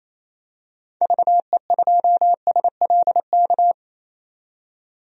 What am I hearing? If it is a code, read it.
Morse code, heard as a keyed tone, about 28 wpm, 700 Hz. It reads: VE2SLK
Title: DXCC-100 Call Signs - 6 Characters - 28wpm